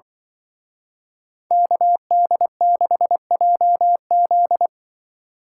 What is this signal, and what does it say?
Morse code, heard as a keyed tone, about 24 wpm, 700 Hz. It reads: KD6JZ
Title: Speed Racing - USA 2x2 Call Signs 20wpm